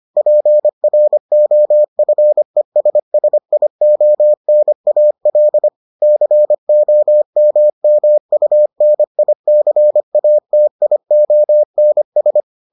Morse code, heard as a keyed tone, about 25 wpm, 600 Hz. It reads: PROFESSIONAL COMMUNICATIONS